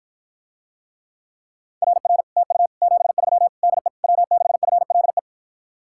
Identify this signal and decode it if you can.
Morse code, heard as a keyed tone, about 54 wpm, 700 Hz. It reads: RR TU 73 DE W6FLE